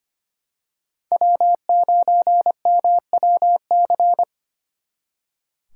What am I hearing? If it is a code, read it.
Morse code, heard as a keyed tone, about 29 wpm, 700 Hz. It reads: W9MWC
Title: Speed Racing - USA 1x3 Call Signs 25wpm